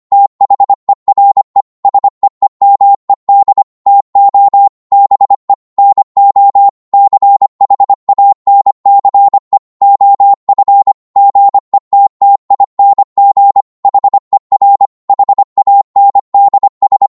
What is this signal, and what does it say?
Morse code, heard as a keyed tone, about 25 wpm, 800 Hz. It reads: THERE SEEMED TO BE NO CHANCE OF GETTING HER HANDS
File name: Morse / psg-25wpm